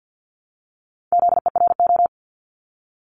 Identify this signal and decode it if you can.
Morse code, heard as a keyed tone, about 50 wpm, 700 Hz. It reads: ZERO
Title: Multi-letter Phonemes Repeated 3x 50wpm